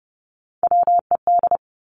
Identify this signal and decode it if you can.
Morse code, heard as a keyed tone, about 30 wpm, 700 Hz. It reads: WED
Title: Three Challenging Letter-Number Combinations 30wpm